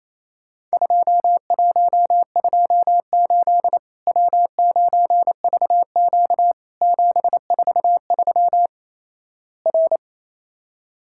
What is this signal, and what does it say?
Morse code, heard as a keyed tone, about 28 wpm, 700 Hz. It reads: 2128 W9VQ 743 R
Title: Worked All Europe DX Contest - QTC Exchange - 28wpm